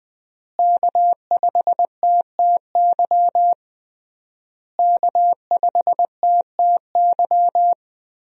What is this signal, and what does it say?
Morse code, heard as a keyed tone, about 20 wpm, 700 Hz. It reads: K5TTY K5TTY
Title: USA Call Signs - 5 Characters - Repeated 3x - 20wpm